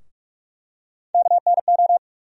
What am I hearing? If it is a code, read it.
Morse code, heard as a keyed tone, about 46 wpm, 700 Hz. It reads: KNO
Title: USA Call Sign Postfixes - 3 Letters 45wpm